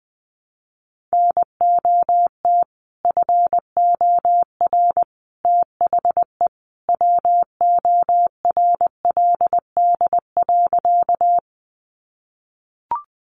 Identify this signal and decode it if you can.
Morse code, heard as a keyed tone, about 20 wpm, 700 Hz. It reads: NOT FOR THE WORLD.
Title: Sentences from Top 600 Words 20wpm